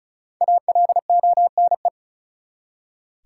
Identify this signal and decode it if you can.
Morse code, heard as a keyed tone, about 36 wpm, 700 Hz. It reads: ALONE
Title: Five Letter Words - 35wpm